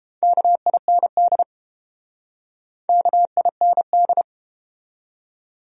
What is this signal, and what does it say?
Morse code, heard as a keyed tone, about 30 wpm, 700 Hz. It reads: KIND KIND